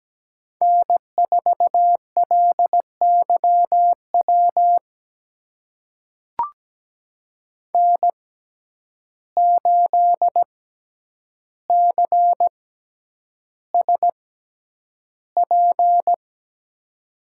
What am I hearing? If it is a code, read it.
Morse code, heard as a keyed tone, about 17 wpm, spaced as for 9 wpm, 700 Hz. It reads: N4LYW N8CSP
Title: USA Call Signs - 5 Characters - 6x Character Spacing - 17wpm